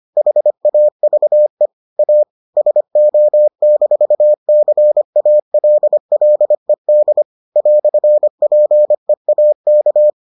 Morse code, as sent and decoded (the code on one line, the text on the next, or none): .... .- ...- . / .- / ... --- -....- -.-. .- .-.. .-.. -.. / .-..-. .--. .- -.-
HAVE A SO-CALLED "PEAK